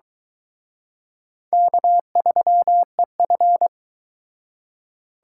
Morse code, heard as a keyed tone, about 23 wpm, 700 Hz. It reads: K3EF